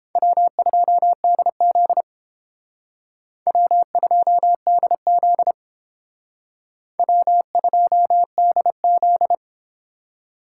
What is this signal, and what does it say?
Morse code, heard as a keyed tone, about 28 wpm, 700 Hz. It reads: W2DZ W2DZ W2DZ